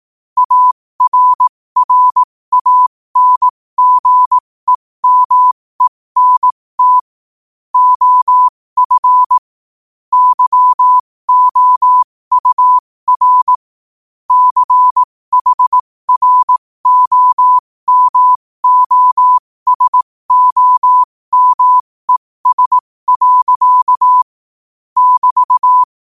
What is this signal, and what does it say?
Morse code, heard as a keyed tone, about 18 wpm, spaced as for 15 wpm, 1000 Hz. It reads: ARRANGEMENT OF YOUR CHROMOSOMES. <BT>
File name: qotdfiles / 15wpm